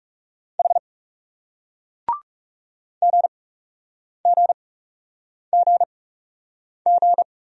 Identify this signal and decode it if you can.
Morse code, heard as a keyed tone, about 41 wpm, spaced as for 8 wpm, 700 Hz. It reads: H GGGG